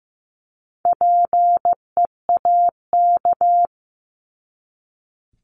Morse code, heard as a keyed tone, about 15 wpm, 700 Hz. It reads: PEAK
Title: Top 1751-2000 Words 15wpm